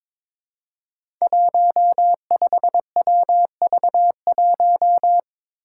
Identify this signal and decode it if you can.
Morse code, heard as a keyed tone, about 22 wpm, 700 Hz. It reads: 15WV1
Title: Five Challenging Letter-Number Combinations 22wpm